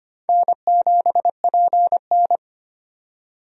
Morse code, heard as a keyed tone, about 25 wpm, 700 Hz. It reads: N7PN
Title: USA Call Signs - 4 Characters - 25wpm